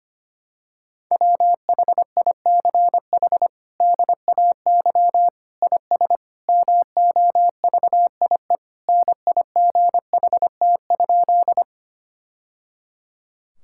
Morse code, heard as a keyed tone, about 25 wpm, 700 Hz. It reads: WHICH DAY IS MOVIE NIGHT?
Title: Sentences from Top 1000 Words - Review Part 5 - 25wpm